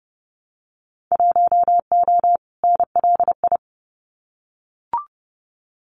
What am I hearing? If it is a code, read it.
Morse code, heard as a keyed tone, about 30 wpm, 700 Hz. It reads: 1O NLI
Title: Winter Field Day Exchanges 30wpm